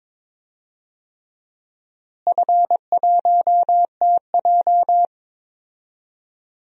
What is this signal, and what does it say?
Morse code, heard as a keyed tone, about 22 wpm, 700 Hz. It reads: F1TJ
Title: DXCC-100 Call Signs - 4 Characters - Repeated 3x - 22wpm